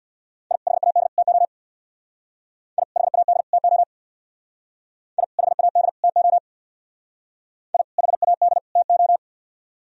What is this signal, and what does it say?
Morse code, heard as a keyed tone, about 61 wpm, 700 Hz. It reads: I HAD TO I HAD TO I HAD TO I HAD TO
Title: Speed-Racing - Sets of 3 Words 50wpm